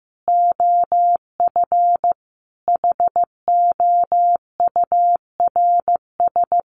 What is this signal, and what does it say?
Morse code, heard as a keyed tone, about 15 wpm, 700 Hz. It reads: OF HOURS
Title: Sets of 5 Words Encore2 15wpm